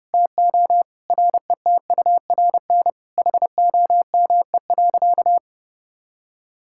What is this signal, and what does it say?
Morse code, heard as a keyed tone, about 30 wpm, 700 Hz. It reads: TO RETURN HOME.
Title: Sentences from Top 1600 Words 30wpm